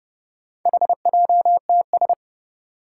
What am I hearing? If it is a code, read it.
Morse code, heard as a keyed tone, about 30 wpm, 700 Hz. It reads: HJTS
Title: Four Challenging Letter-Number Combinations 30wpm